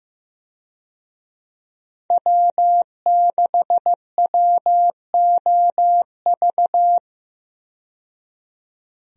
Morse code, heard as a keyed tone, about 15 wpm, 700 Hz. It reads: W6WOV
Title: USA Call Signs - 5 Characters - 15wpm